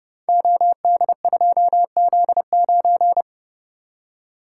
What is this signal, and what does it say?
Morse code, heard as a keyed tone, about 28 wpm, 700 Hz. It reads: OD2Z9
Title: Code Groups - 5 Letter:Numbers - 30wpm